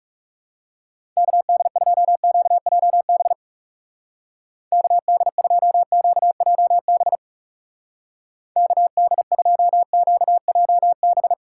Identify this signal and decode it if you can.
Morse code, heard as a keyed tone, about 37 wpm, 700 Hz. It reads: KD2QJB KD2QJB KD2QJB
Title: Speed Racing - USA 2x3 Call Signs 30wpm